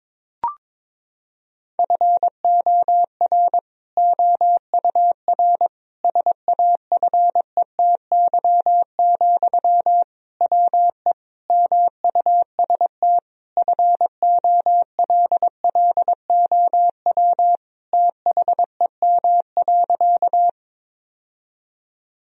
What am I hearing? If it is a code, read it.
Morse code, heard as a keyed tone, about 22 wpm, 700 Hz. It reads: FOR OUR SAFETY, WE MUST FOLLOW THEM.
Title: Sentences from Top 1000 Words - Review Part 4 - 22wpm